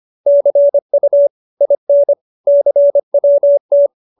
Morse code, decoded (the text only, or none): CU IN CWT